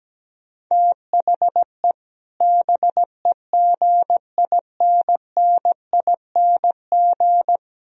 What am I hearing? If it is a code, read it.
Morse code, heard as a keyed tone, about 17 wpm, 700 Hz. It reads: THE BEGINNING
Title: Speed-Racing - Sets of 3 Words 17wpm